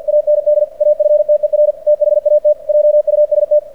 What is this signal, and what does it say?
Morse code, heard as a keyed tone, about 46 wpm, 600 Hz. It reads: AND AFTER THAT JUST
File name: archived / CRSnet